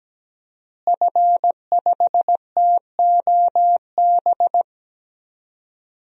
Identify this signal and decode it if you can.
Morse code, heard as a keyed tone, about 17 wpm, 700 Hz. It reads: F5TOB
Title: DXCC-100 Call Signs - 5 Characters - Repeated 3x - 17wpm